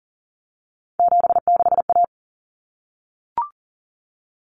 Morse code, heard as a keyed tone, about 40 wpm, 700 Hz. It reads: 76A
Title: Common Cut Numbers - 3-Digit 40wpm